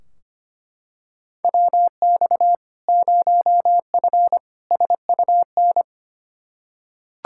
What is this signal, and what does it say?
Morse code, heard as a keyed tone, about 25 wpm, 700 Hz. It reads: WX 0F SUN
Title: Practice QSO - Exchange 2 - Info - 25wpm